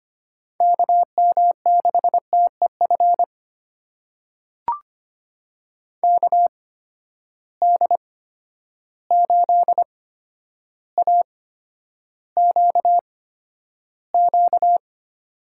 Morse code, heard as a keyed tone, about 25 wpm, spaced as for 11 wpm, 700 Hz. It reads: KM6TEF KD8AQQ
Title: USA Call Signs - 6 Characters - 8x Character Spacing - 25wpm